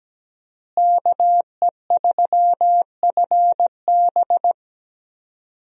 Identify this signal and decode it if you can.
Morse code, heard as a keyed tone, about 17 wpm, 700 Hz. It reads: KE3FB
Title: USA Call Signs - 5 Characters - Repeated 2x - 17wpm